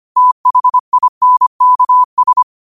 Morse code, heard as a keyed tone, about 25 wpm, 1000 Hz. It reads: THINKS
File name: qotdfiles / 25wpm